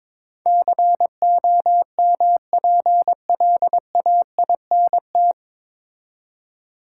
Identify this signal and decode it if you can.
Morse code, heard as a keyed tone, about 22 wpm, 700 Hz. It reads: COMPLAINT
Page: Morse Code Ninja practice 22wpm